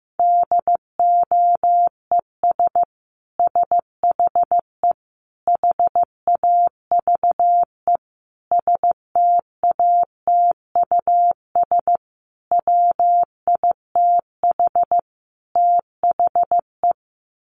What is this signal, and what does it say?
Morse code, heard as a keyed tone, about 14 wpm, 700 Hz. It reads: DOES SHE HAVE STATUS WITH THE
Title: Sentences from Top 1400 Words 15wpm